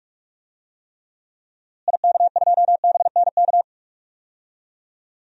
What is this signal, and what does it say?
Morse code, heard as a keyed tone, about 46 wpm, 700 Hz. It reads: IK2BNK
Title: DXCC-100 Call Signs - 6 Characters - Repeated 2x - 45wpm